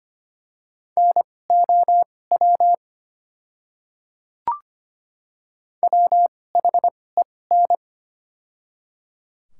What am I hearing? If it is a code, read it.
Morse code, heard as a keyed tone, about 25 wpm, spaced as for 18 wpm, 700 Hz. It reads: NOW WHEN
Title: 2x Character Spacing - Top 300 Words - 25wpm